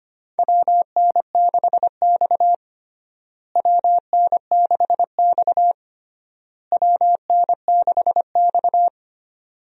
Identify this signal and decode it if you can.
Morse code, heard as a keyed tone, about 25 wpm, 700 Hz. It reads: WN6X WN6X WN6X
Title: USA Call Signs - 4 Characters - Repeated 3x - 25wpm